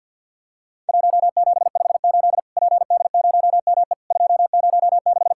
Morse code, heard as a keyed tone, about 50 wpm, 700 Hz. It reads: 1758 PD0ME 206
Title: Worked All Europe DX Contest - QTC Exchange with Repeats - 50wpm